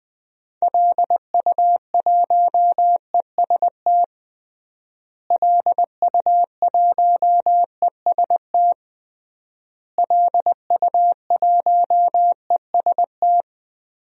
Morse code, heard as a keyed tone, about 20 wpm, 700 Hz. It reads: LU1EST LU1EST LU1EST
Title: DXCC-100 Call Signs - 6 Characters - Repeated 3x - 20wpm